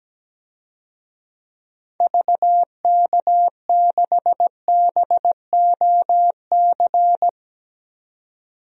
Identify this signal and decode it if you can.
Morse code, heard as a keyed tone, about 17 wpm, 700 Hz. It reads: VK6BOC
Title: DXCC-100 Call Signs - 6 Characters - 17wpm